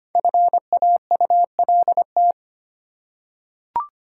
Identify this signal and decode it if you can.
Morse code, heard as a keyed tone, about 25 wpm, 700 Hz. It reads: FAULT
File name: 5 Letter Words - First 3 Letters Given - 25wpm